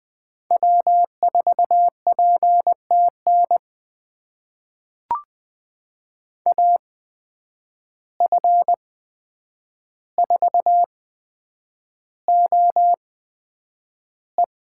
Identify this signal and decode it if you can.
Morse code, heard as a keyed tone, about 21 wpm, spaced as for 9 wpm, 700 Hz. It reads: W4PTN AF4OE
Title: USA Call Signs - 5 Characters - 8x Character Spacing - 20wpm